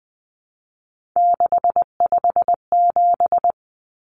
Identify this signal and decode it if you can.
Morse code, heard as a keyed tone, about 20 wpm, 700 Hz. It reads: 657
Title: Common Cut Numbers - 3-Digit 20wpm